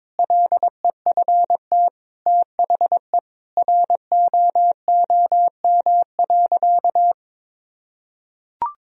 Morse code, heard as a keyed tone, about 22 wpm, 700 Hz. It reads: LEFT THE ROOM.
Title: Sentences from Top 1000 Words - Review Part 1 - 22wpm